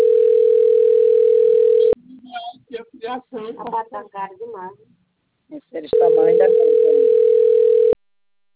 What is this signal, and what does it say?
Ringback tone (North America), 440 + 480 Hz, with speech.
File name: rg-1600-unknown- - - .89531